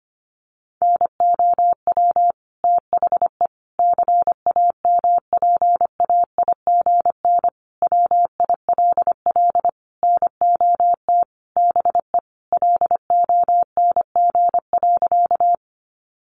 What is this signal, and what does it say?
Morse code, heard as a keyed tone, about 25 wpm, 700 Hz. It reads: NOW THE CAMPAIGN WILL NOT BE LONG.
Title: Sentences from Top 1200 Words 25wpm